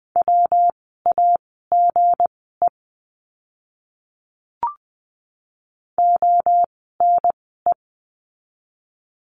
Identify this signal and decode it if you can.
Morse code, heard as a keyed tone, about 20 wpm, spaced as for 15 wpm, 700 Hz. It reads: WAGE ONE
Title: 160 Lesson 26f - Letter g - Words 20wpm